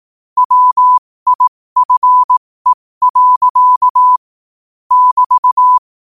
Morse code, heard as a keyed tone, about 18 wpm, spaced as for 15 wpm, 1000 Hz. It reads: WIFE. <BT>
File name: qotdfiles / 15wpm